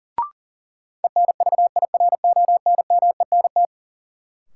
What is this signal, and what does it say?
Morse code, heard as a keyed tone, about 40 wpm, 700 Hz. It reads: ENVIRONMENT
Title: Sentences from Top 1000 Words - Review Part 5 - New Word Drill - 40wpm